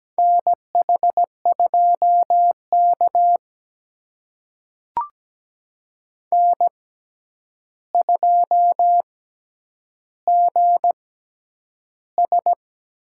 Morse code, heard as a keyed tone, about 17 wpm, spaced as for 9 wpm, 700 Hz. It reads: NH2K N2GS
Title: USA Call Signs - 4 Characters - 6x Character Spacing - 17wpm